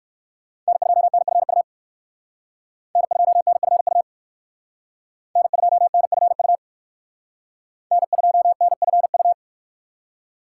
Morse code, heard as a keyed tone, about 59 wpm, 700 Hz. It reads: N2NFU N2NFU N2NFU N2NFU